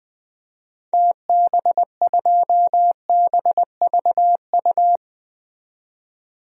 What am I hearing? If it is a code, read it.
Morse code, heard as a keyed tone, about 20 wpm, 700 Hz. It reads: TB2BVU
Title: DXCC-100 Call Signs - 6 Characters - 20wpm